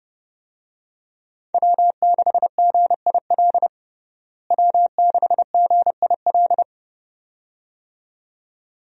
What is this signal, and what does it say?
Morse code, heard as a keyed tone, about 30 wpm, 700 Hz. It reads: W6GIL W6GIL
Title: USA Call Signs - 5 Characters - Repeated 2x - 30wpm